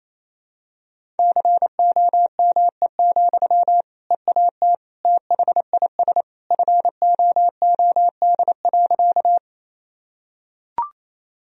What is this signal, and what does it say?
Morse code, heard as a keyed tone, about 28 wpm, 700 Hz. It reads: COME, EAT THIS FOOD.
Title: Sentences from Top 1000 Words - Review Part 1 - 28wpm